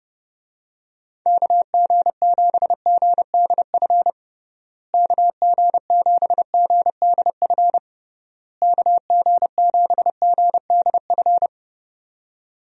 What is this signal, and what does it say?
Morse code, heard as a keyed tone, about 30 wpm, 700 Hz. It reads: KG7GDF KG7GDF KG7GDF
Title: USA Call Signs - 6 Characters - Repeated 3x - 30wpm